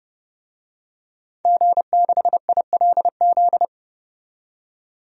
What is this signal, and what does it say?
Morse code, heard as a keyed tone, about 30 wpm, 700 Hz. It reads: G6ILZ
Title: DXCC-100 Call Signs - 5 Characters - 30wpm